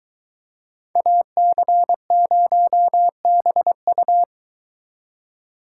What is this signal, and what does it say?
Morse code, heard as a keyed tone, about 22 wpm, 700 Hz. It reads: AC0BU